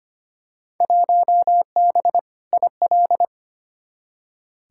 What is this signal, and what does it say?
Morse code, heard as a keyed tone, about 25 wpm, 700 Hz. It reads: 1B IL
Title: Field Day Exchanges - 25wpm